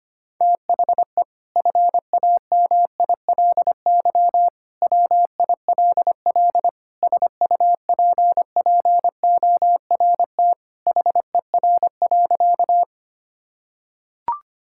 Morse code, heard as a keyed tone, about 25 wpm, 700 Hz. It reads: THE FAMILY WILL SUPPORT HER.
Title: Sentences from Top 1000 Words - Review Part 5 - 25wpm